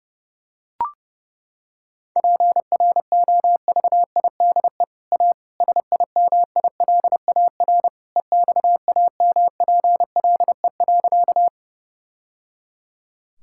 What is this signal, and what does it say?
Morse code, heard as a keyed tone, about 30 wpm, 700 Hz. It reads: PROVIDE A SIMILAR EXAMPLE.
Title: Sentences from Top 1000 Words - Review Part 5 - 30wpm